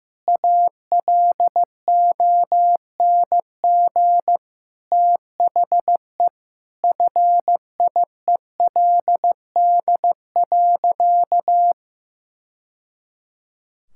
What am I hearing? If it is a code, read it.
Morse code, heard as a keyed tone, about 15 wpm, 700 Hz. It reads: ALONG THE FIELD.
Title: Sentences from Top 1000 Words - Review Part 5 - 15wpm